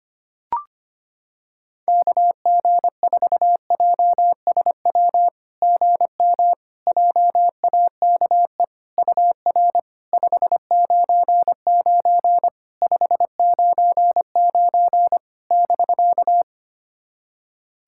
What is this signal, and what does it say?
Morse code, heard as a keyed tone, about 25 wpm, 690 Hz. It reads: KG4JSW GM JAKE UR 599 599 <BK>
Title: SOTA QSO Practice 25wpm